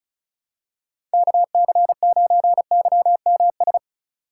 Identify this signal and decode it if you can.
Morse code, heard as a keyed tone, about 36 wpm, 700 Hz. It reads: KC9YMS
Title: USA Call Signs - 6 Characters - 3x Character Spacing - 35wpm